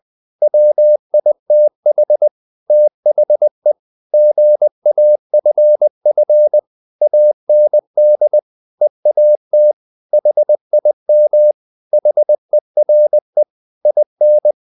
Morse code, heard as a keyed tone, about 20 wpm, 600 Hz. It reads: WITH THE GAFF AND EAT HIM HERE IN
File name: the old man and the sea / the old man and the sea part 1 20wpm